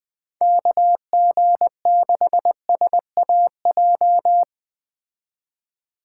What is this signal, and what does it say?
Morse code, heard as a keyed tone, about 20 wpm, 700 Hz. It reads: KG6SAJ